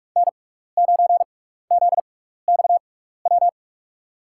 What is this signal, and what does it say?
Morse code, heard as a keyed tone, about 45 wpm, spaced as for 15 wpm, 700 Hz. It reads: N9ZXW
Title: USA Call Signs - 5 Characters - 6x Character Spacing - 45wpm